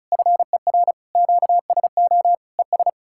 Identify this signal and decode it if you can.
Morse code, heard as a keyed tone, about 48 wpm, spaced as for 42 wpm, 700 Hz. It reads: FER QSO ES
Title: Speed-Racing Common QSO Phrases - 35wpm